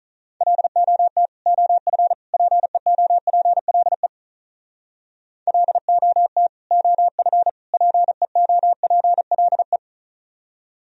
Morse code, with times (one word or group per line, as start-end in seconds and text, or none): 0.4-1.3 LOT
1.5-2.1 OF
2.3-4.1 PEOPLE
5.5-6.5 LOT
6.7-7.5 OF
7.7-9.8 PEOPLE